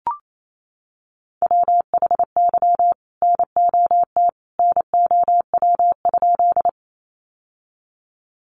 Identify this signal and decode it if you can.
Morse code, heard as a keyed tone, about 28 wpm, 700 Hz. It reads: WHY NOT NOW?
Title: Sentences from Top 600 Words 28wpm